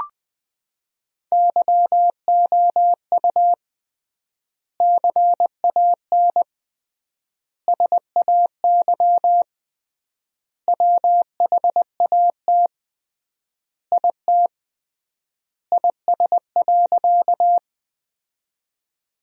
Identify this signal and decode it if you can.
Morse code, heard as a keyed tone, about 21 wpm, 700 Hz. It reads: YOU CAN SAY WHAT IT IS.
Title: 3x Word-Spacing - Sentences from Top 200 Words - 20wpm